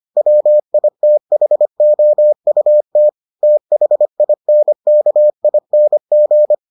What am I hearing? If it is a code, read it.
Morse code, heard as a keyed tone, about 25 wpm, 600 Hz. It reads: WITHOUT THINKING